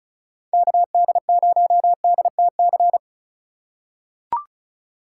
Morse code, heard as a keyed tone, about 36 wpm, 700 Hz. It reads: KD0DTC
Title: USA Call Signs - 6 Characters - 8x Character Spacing - 35wpm